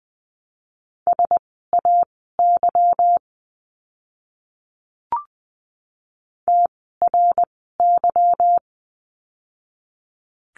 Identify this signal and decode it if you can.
Morse code, heard as a keyed tone, about 20 wpm, spaced as for 15 wpm, 700 Hz. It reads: SAY TRY
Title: 160 Lesson 26f - Letter g - Words 20wpm